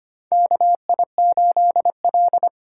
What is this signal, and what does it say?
Morse code, heard as a keyed tone, about 25 wpm, 700 Hz. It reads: KI8L
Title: USA Call Signs - 4 Characters - 2x Character Spacing - 25wpm